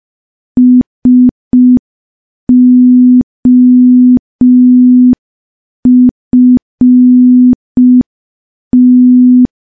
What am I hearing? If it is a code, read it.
Morse code, heard as a keyed tone, about 5 wpm, 250 Hz. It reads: SOFT
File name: morse / morse-training-05WPM